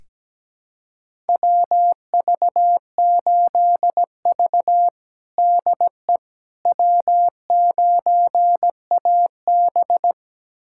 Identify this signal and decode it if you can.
Morse code, heard as a keyed tone, about 17 wpm, 700 Hz. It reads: WV8V DE W9AB
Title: USA Callsign DE Callsign - 4 Characters - 17wpm